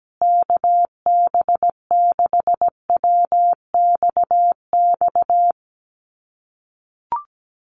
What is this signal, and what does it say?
Morse code, heard as a keyed tone, about 17 wpm, 700 Hz. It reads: KB6WXX
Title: USA Call Signs Repeated 2x 17wpm